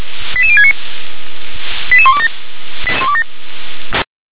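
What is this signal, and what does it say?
Five-tone selective call (ZVEI), heard as quick runs of five tones, tones of about 80 ms. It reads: 88069 88127
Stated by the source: ZVEI five-tone call 88186 received by SDR.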